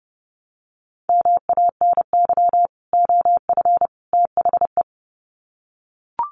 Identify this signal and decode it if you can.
Morse code, heard as a keyed tone, about 30 wpm, 700 Hz. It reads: MANY OF THE
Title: Sets of 3 Words 30wpm